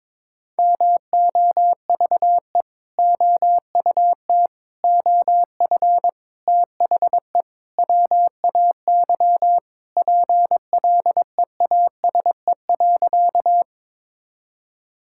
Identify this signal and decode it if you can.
Morse code, heard as a keyed tone, about 22 wpm, 700 Hz. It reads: MOVE OUT OF THE WAY PLEASE.
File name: practice / 22wpm / Sentences with 4-Word Phrases - 22wpm